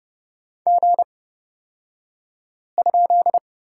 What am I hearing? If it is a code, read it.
Morse code, heard as a keyed tone, about 29 wpm, spaced as for 5 wpm, 700 Hz. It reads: G?
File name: Lesson 38d - Letter q - Two Characters 30wpm